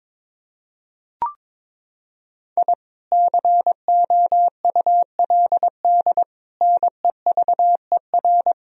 Morse code, heard as a keyed tone, about 22 wpm, 700 Hz. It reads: I COULD NEVER
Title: Sentences from Top 1000 Words - Review Part 3 - 22wpm